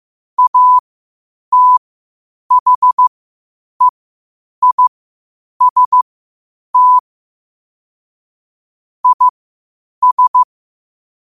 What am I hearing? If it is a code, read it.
Morse code, heard as a keyed tone, about 15 wpm, spaced as for 8 wpm, 1000 Hz. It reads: ATHEIST IS
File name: qotdfiles / 8wpm